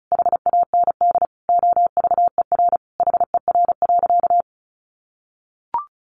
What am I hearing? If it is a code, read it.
Morse code, heard as a keyed tone, about 34 wpm, 700 Hz. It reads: HAND OVER HER.
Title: Sentences from Top 1400 Words 35wpm